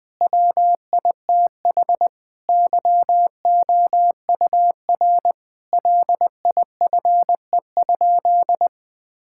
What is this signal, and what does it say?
Morse code, heard as a keyed tone, about 20 wpm, 700 Hz. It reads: WITH YOUR LIFE?
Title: Sentences from Top 1000 Words - Review Part 6 - 20wpm